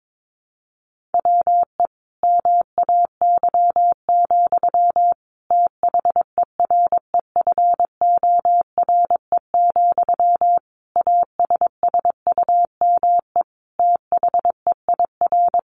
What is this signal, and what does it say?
Morse code, heard as a keyed tone, about 22 wpm, 700 Hz. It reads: WE MAY, THEREFORE, ASSUME THEIR
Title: Sentences from Top 1500 Words 22wpm